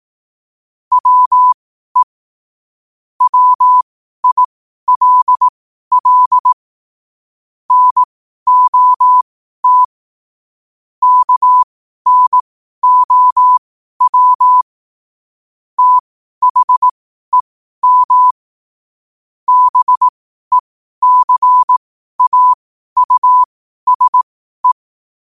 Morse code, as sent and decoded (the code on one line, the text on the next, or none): .-- . / .-- .. .-.. .-.. / -. --- - / -.- -. --- .-- / - .... -- / -... -.-. .- ..- ... .
WE WILL NOT KNOW THEM BECAUSE